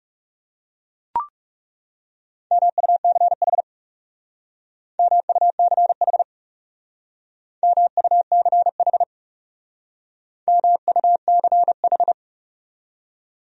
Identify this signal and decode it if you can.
Morse code, heard as a keyed tone, about 38 wpm, 700 Hz. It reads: MUCH MUCH MUCH MUCH